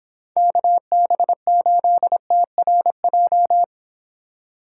Morse code, heard as a keyed tone, about 31 wpm, 700 Hz. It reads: KB8TRJ